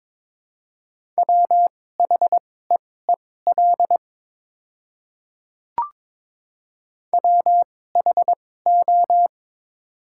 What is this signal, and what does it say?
Morse code, heard as a keyed tone, about 22 wpm, spaced as for 16 wpm, 700 Hz. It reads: WHEEL WHO
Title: 2x Character Spacing - Top 500 Words - 22wpm